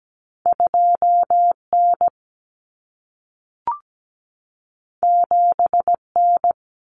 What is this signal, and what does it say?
Morse code, heard as a keyed tone, about 17 wpm, 700 Hz. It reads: 2N 7N
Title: Common Cut Numbers - 2-Digit 17wpm